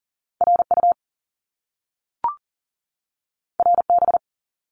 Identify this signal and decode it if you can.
Morse code, heard as a keyed tone, about 38 wpm, 700 Hz. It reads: RU RB
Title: Top Two Letter Combinations in Words 40wpm